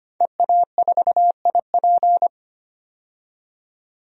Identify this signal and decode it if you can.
Morse code, heard as a keyed tone, about 25 wpm, 700 Hz. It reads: EA4IP
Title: DXCC-100 Call Signs - 5 Characters - 25wpm